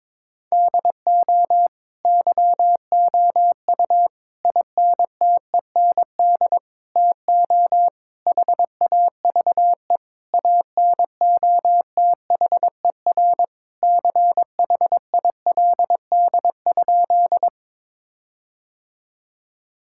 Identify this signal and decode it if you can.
Morse code, heard as a keyed tone, about 22 wpm, 700 Hz. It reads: DO YOU INTEND TO HAVE ANOTHER CHILD?
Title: Sentences from Top 1600 Words 22wpm